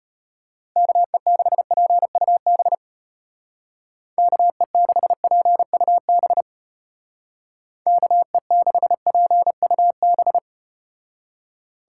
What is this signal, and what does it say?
Morse code, heard as a keyed tone, about 34 wpm, 700 Hz. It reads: KE6PUB KE6PUB KE6PUB